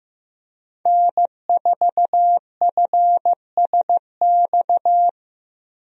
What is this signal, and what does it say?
Morse code, heard as a keyed tone, about 15 wpm, 700 Hz. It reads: N4FSX
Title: USA Call Signs - 5 Characters - Repeated 3x - 15wpm